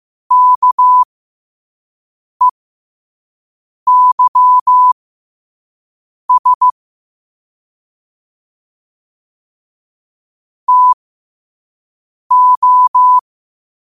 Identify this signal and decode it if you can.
Morse code, heard as a keyed tone, about 15 wpm, spaced as for 5 wpm, 1000 Hz. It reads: KEYS TO
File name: qotdfiles / 5wpm